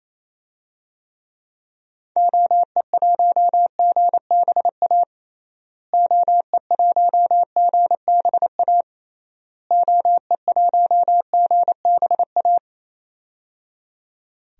Morse code, heard as a keyed tone, about 28 wpm, 700 Hz. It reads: OE1GBA OE1GBA OE1GBA
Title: DXCC-100 Call Signs - 6 Characters - Repeated 3x - 28wpm